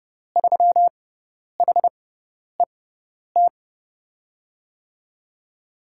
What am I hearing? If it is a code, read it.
Morse code, heard as a keyed tone, about 30 wpm, spaced as for 10 wpm, 700 Hz. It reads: 3HET
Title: USA Call Signs - 5 Characters - 6x Character Spacing - 30wpm